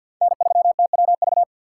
Morse code, heard as a keyed tone, about 48 wpm, 700 Hz. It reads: N3TWV
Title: USA Call Signs - 5 Characters - 6x Character Spacing - 50wpm